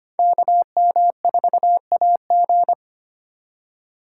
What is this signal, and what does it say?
Morse code, heard as a keyed tone, about 25 wpm, 700 Hz. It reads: KM4AG
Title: USA Call Signs - 5 Characters - 4x Character Spacing - 25wpm